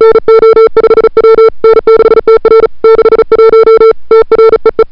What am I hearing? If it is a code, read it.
Morse code, heard as a keyed tone, about 35 wpm, 440 Hz. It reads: NO5W N6TR 61 TREE
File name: n6tr / Fab2007CWSprintHour1